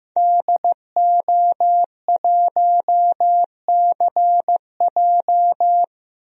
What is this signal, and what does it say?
Morse code, heard as a keyed tone, about 15 wpm, 700 Hz. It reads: DO1CJ
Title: DXCC-100 Call Signs - 5 Characters - 15wpm